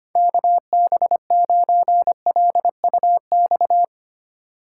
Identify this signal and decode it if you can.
Morse code, heard as a keyed tone, about 31 wpm, 700 Hz. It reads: KB9LUX
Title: Speed Racing - USA 2x3 Call Signs 25wpm